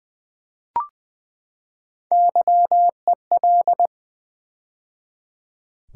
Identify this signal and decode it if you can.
Morse code, heard as a keyed tone, about 20 wpm, 700 Hz. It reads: YEL
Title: Build A Word - 3 Letters At A Time - 6-Letter Words - 20wpm